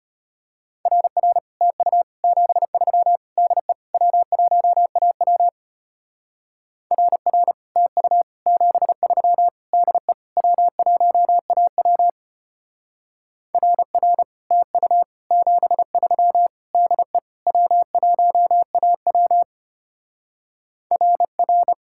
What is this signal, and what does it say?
Morse code, heard as a keyed tone, about 32 wpm, 700 Hz. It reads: RR TU 73 DE W1AW RR TU 73 DE W1AW RR TU 73 DE W1AW RR